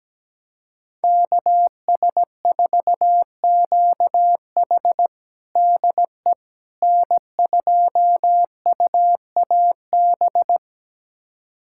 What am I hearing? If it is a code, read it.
Morse code, heard as a keyed tone, about 17 wpm, 700 Hz. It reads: KS4QH DE N2UAB
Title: USA Callsign DE Callsign - 5 Characters - 17wpm